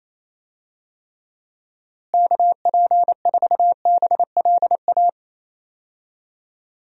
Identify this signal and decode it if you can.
Morse code, heard as a keyed tone, about 28 wpm, 700 Hz. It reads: KP4BLA
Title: DXCC-100 Call Signs - 6 Characters - Repeated 3x - 28wpm